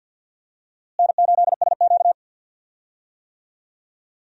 Morse code, heard as a keyed tone, about 52 wpm, 700 Hz. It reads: N8SQ
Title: USA Call Signs - 4 Characters - Repeated 2x - 50wpm